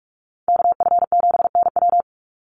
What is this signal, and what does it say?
Morse code, heard as a keyed tone, about 44 wpm, 700 Hz. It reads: KF7NW
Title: USA Call Signs 45wpm